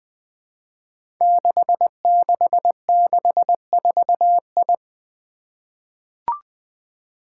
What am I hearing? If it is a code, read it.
Morse code, heard as a keyed tone, about 20 wpm, 700 Hz. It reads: 6664I
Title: Five Challenging Letter-Number Combinations 20wpm